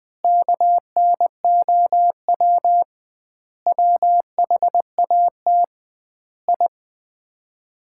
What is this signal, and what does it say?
Morse code, heard as a keyed tone, about 20 wpm, 700 Hz. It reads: KNOW WHAT I
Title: 2x Word-Spacing - Sets of 5 Words - 20wpm